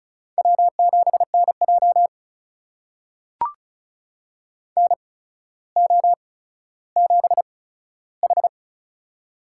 Morse code, heard as a keyed tone, about 36 wpm, spaced as for 15 wpm, 700 Hz. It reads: W7NJ NO7H
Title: USA Call Signs - 4 Characters - 8x Character Spacing - 35wpm